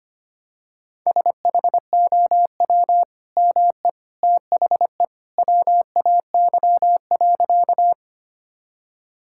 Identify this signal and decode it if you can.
Morse code, heard as a keyed tone, about 25 wpm, 700 Hz. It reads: SHOW ME THE WAY.
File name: practice / 25wpm / Sentences with 2-Word Phrases - 25wpm